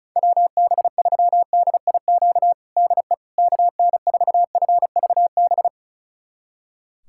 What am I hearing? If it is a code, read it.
Morse code, heard as a keyed tone, about 34 wpm, 700 Hz. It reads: WB3DIQ DE KN4FVB
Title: USA Callsign DE Callsign - 6 Characters - 35wpm